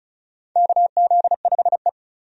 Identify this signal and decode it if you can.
Morse code, heard as a keyed tone, about 36 wpm, 700 Hz. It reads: KZ5E
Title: USA Call Signs - 4 Characters - 35wpm